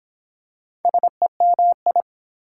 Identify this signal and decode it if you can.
Morse code, heard as a keyed tone, about 23 wpm, 700 Hz. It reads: SEMI